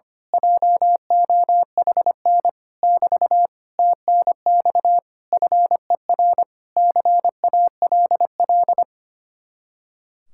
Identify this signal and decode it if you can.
Morse code, heard as a keyed tone, about 24 wpm, 700 Hz. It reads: JOHN <BT> TNX FER CALL
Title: Speed-Racing Common QSO Phrases - 25wpm